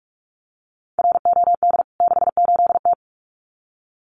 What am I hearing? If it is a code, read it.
Morse code, heard as a keyed tone, about 44 wpm, 700 Hz. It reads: ROD 68T
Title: CWT Exchange 45wpm